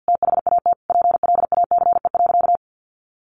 Morse code, heard as a keyed tone, about 50 wpm, 700 Hz. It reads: THAT PLACE.
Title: Sentences from Top 800 Words 50wpm